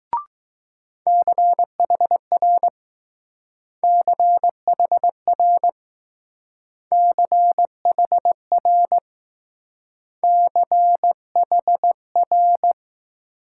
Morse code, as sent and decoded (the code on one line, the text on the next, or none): -.-. .... .-. / -.-. .... .-. / -.-. .... .-. / -.-. .... .-.
CHR CHR CHR CHR